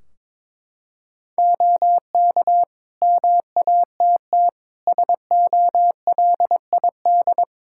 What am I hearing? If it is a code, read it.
Morse code, heard as a keyed tone, about 22 wpm, 700 Hz. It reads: OK MATT SOLID
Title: Practice QSO - Exchange 3 - More Info - 22wpm